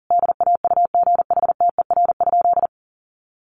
Morse code, heard as a keyed tone, about 40 wpm, 700 Hz. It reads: DAUGHTER?
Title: Sentences from Top 1500 Words 40wpm